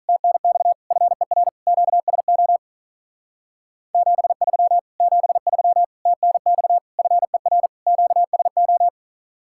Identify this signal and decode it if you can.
Morse code, heard as a keyed tone, about 46 wpm, 700 Hz. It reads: TNX FER QSO 73 73 TNX FER QSO